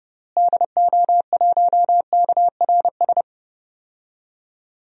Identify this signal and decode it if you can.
Morse code, heard as a keyed tone, about 29 wpm, 700 Hz. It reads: DO1KRS
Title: DXCC-100 Call Signs - 6 Characters - Repeated 3x - 30wpm